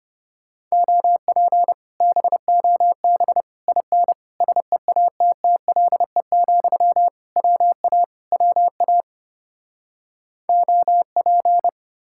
Morse code, heard as a keyed tone, about 33 wpm, 700 Hz. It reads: OP BOB IN SEATTLE, WA WA OP